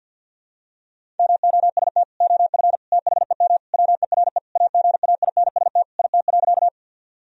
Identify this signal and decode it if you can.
Morse code, heard as a keyed tone, about 51 wpm, 700 Hz. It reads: MOST OF THEM WERE AGAINST IT.
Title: Sentences from Top 1000 Words - Review Part 1 - 50wpm